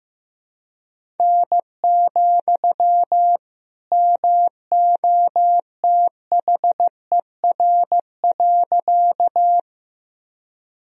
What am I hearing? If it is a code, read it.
Morse code, heard as a keyed tone, about 15 wpm, 700 Hz. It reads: N, MOTHER.
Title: Sentences from Top 200 Words and Abbreviations 15wpm